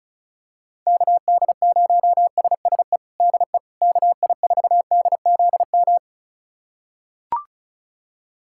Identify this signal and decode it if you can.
Morse code, heard as a keyed tone, about 36 wpm, 700 Hz. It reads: KD0SSE DE KI4DZM
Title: USA Callsign DE Callsign 35wpm